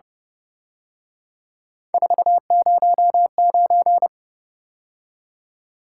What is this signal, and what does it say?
Morse code, heard as a keyed tone, about 30 wpm, 700 Hz. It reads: 409
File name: Speed-Racing - Three Letter-Number Combinations - 25wpm